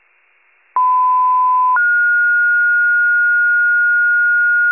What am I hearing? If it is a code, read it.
Two-tone page, heard as a one-second tone, then a three-second tone, about 1010 then 1430 Hz. It is heard from 0.8 to 4.7 s.